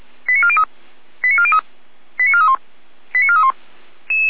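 Five-tone selective call (ZVEI), heard as quick runs of five tones, tones of about 70 ms. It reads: 89403 89403 89431 89431
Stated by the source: ZVEI five-tone selective call 89711 received on SDR.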